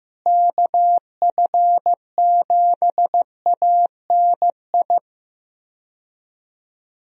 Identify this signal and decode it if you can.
Morse code, heard as a keyed tone, about 15 wpm, 700 Hz. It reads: KF7ANI
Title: USA Call Signs - 6 Characters - Repeated 2x - 15wpm